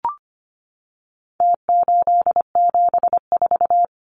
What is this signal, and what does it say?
Morse code, heard as a keyed tone, about 25 wpm, 700 Hz. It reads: T874
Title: Numbers - Contest Style - 4-Digit 25wpm